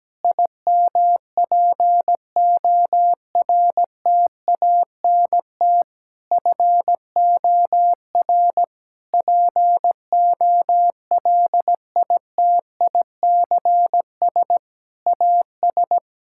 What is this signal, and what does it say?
Morse code, heard as a keyed tone, about 17 wpm, 700 Hz. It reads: IMPORTANT FOR POLITICS AS
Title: Sentences from Top 1000 Words - Review Part 6 - 17wpm